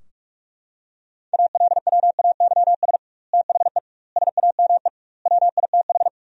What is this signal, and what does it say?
Morse code, heard as a keyed tone, about 42 wpm, 700 Hz. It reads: ALWAYS THE SAME WITH
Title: 2x Word-Spacing - Sentences from Top 300 Words - 45wpm